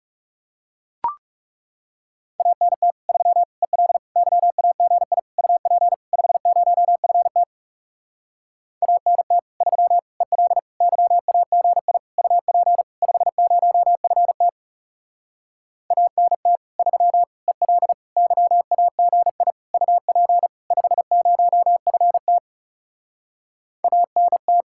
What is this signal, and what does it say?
Morse code, heard as a keyed tone, about 40 wpm, 700 Hz. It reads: ANT 3 EL YAGI UP 50FT ANT 3 EL YAGI UP 50FT ANT 3 EL YAGI UP 50FT ANT